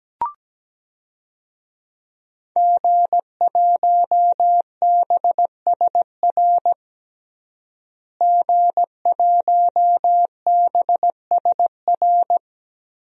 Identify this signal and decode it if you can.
Morse code, heard as a keyed tone, about 17 wpm, 700 Hz. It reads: G1BSR G1BSR
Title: DXCC-100 Call Signs - 5 Characters - Repeated 3x - 17wpm